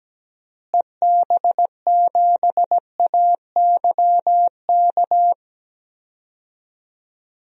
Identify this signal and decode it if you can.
Morse code, heard as a keyed tone, about 17 wpm, 700 Hz. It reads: EB7AYK
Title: DXCC-100 Call Signs - 6 Characters - Repeated 3x - 17wpm